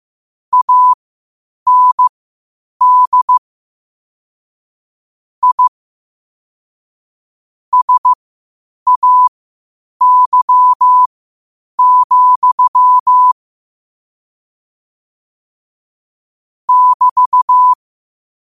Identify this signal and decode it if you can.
Morse code, heard as a keyed tone, about 15 wpm, spaced as for 8 wpm, 1000 Hz. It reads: AND I SAY, <BT>
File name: qotdfiles / 8wpm